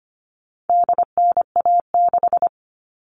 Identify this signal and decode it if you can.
Morse code, heard as a keyed tone, about 25 wpm, 700 Hz. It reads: DNA6
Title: All Cut Numbers - 4-Digit 25wpm